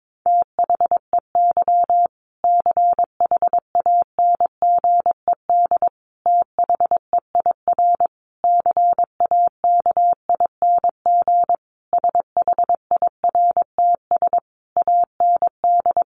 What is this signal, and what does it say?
Morse code, heard as a keyed tone, about 22 wpm, 700 Hz. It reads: THEY CHANGED THEIR CAKING SHIRTS AND
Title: A Sound Of Thunder 22wpm